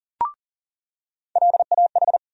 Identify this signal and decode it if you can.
Morse code, heard as a keyed tone, about 40 wpm, 700 Hz. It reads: LAH
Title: USA Call Sign Postfixes - 3 Letters 40wpm